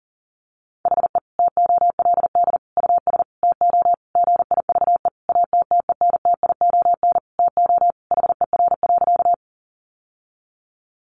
Morse code, heard as a keyed tone, about 37 wpm, 700 Hz. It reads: HE TOLD US TO GIVE ATTENTION TO HER.
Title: Sentences from Top 1000 Words 40wpm